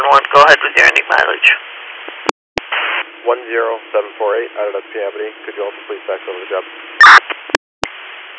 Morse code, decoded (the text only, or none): none